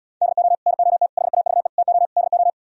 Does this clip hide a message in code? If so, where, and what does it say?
Yes: Morse code from 0.2 to 2.5 s, heard as a keyed tone, about 77 wpm, 700 Hz. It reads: DO NOT HAVE TO DO